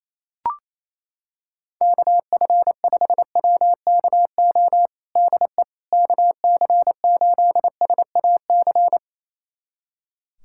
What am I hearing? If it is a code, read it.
Morse code, heard as a keyed tone, about 28 wpm, 700 Hz. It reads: KF5WKO DE KC8SAC
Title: USA Callsign DE Callsign 28wpm